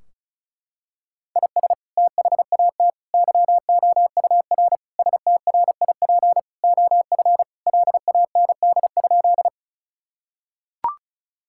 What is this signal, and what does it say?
Morse code, heard as a keyed tone, about 34 wpm, 700 Hz. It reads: IS THAT YOUR STRIP OF LAND?
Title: Sentences from Top 1600 Words 35wpm